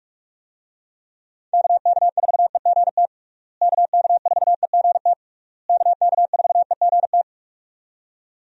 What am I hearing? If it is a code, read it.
Morse code, heard as a keyed tone, about 46 wpm, 700 Hz. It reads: KK4EGT KK4EGT KK4EGT
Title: USA Call Signs - 6 Characters - Repeated 3x - 45wpm